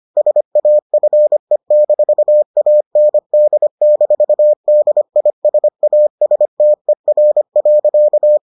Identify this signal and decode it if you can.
Morse code, heard as a keyed tone, about 25 wpm, 600 Hz. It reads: SAFE-AND-DISASTER.